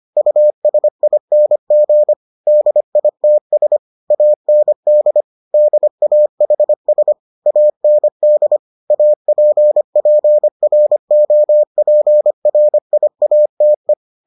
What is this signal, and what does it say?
Morse code, heard as a keyed tone, about 25 wpm, 600 Hz. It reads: USING DITS AND DAHS AND APPROPRIATE